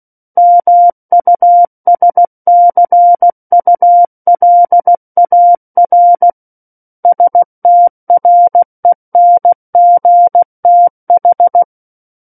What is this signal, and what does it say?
Morse code, heard as a keyed tone, about 16 wpm, 700 Hz. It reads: MUSCULAR STRENGTH
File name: Book One / 07-Chapter 7 - 16 WPM